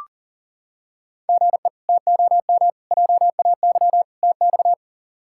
Morse code, heard as a keyed tone, about 40 wpm, 700 Hz. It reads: GE TOM JAY TX
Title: K1USN SST QSO Practice - 40wpm